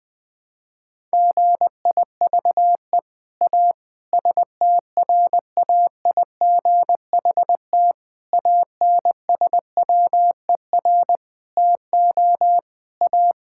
Morse code, heard as a keyed tone, about 20 wpm, 700 Hz. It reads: GIVE A STRAIGHT ANSWER TO A
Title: Sentences from Top 1000 Words - Review Part 6 - 20wpm